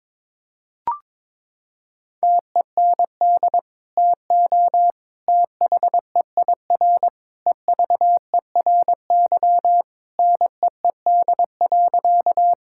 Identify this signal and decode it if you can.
Morse code, heard as a keyed tone, about 22 wpm, 700 Hz. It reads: TEND TO THEIR EVERY NEED.
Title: Sentences from Top 1000 Words - Review Part 5 - 22wpm